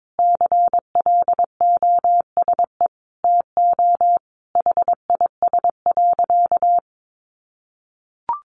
Morse code, heard as a keyed tone, about 22 wpm, 700 Hz. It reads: CLOSE TO HIS.
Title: Sentences from Top 900 Words 22wpm